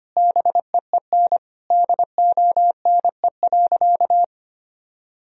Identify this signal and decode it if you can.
Morse code, heard as a keyed tone, about 25 wpm, 700 Hz. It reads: BEEN DONE.
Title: Sentences from Top 1000 Words - Review Part 1 - 25wpm